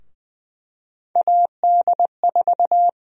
Morse code, heard as a keyed tone, about 20 wpm, 700 Hz. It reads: AD4
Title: USA Call Sign Prefixes - Repeated 2x - 20wpm